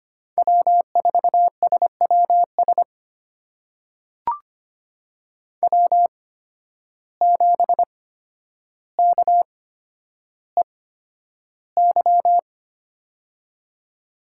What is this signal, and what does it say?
Morse code, heard as a keyed tone, about 25 wpm, spaced as for 11 wpm, 700 Hz. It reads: W4SWS W7KEY
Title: USA Call Signs - 5 Characters - 8x Character Spacing - 25wpm